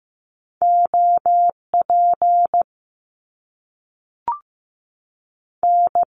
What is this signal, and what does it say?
Morse code, heard as a keyed tone, about 15 wpm, 700 Hz. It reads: OP N
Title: Top Two Letter Combinations in Words 15wpm